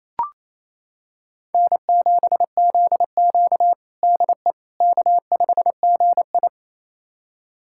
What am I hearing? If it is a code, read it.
Morse code, heard as a keyed tone, about 28 wpm, 700 Hz. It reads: N7ZQ DE K5GI
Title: USA Callsign DE Callsign - 4 Characters - 28wpm